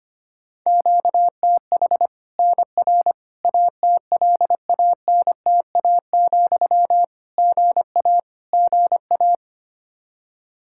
Morse code, heard as a keyed tone, about 25 wpm, 700 Hz. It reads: QTH NR ATLANTA, GA GA
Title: Practice QSO - Exchange 1 - Beginning - 25wpm